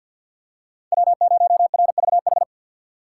Morse code, heard as a keyed tone, about 46 wpm, 700 Hz. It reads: W0RVH
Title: USA Call Signs - 5 Characters - 2x Character Spacing - 50wpm